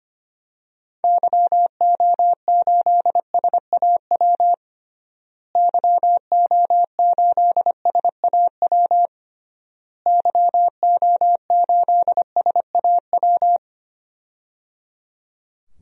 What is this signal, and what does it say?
Morse code, heard as a keyed tone, about 25 wpm, 700 Hz. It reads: YO8SAW YO8SAW YO8SAW
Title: DXCC-100 Call Signs - 6 Characters - Repeated 3x - 25wpm